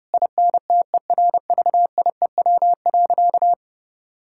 Morse code, heard as a keyed tone, about 30 wpm, 700 Hz. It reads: INTERVIEW.
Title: Sentences from Top 1600 Words 30wpm